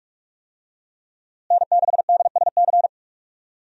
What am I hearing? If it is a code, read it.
Morse code, heard as a keyed tone, about 43 wpm, 700 Hz. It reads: N6DSC
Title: USA Call Signs - 5 Characters - Repeated 2x - 45wpm